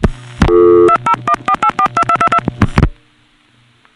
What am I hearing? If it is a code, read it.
DTMF, heard as telephone touch tones, tones of about 70 ms. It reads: 2014842222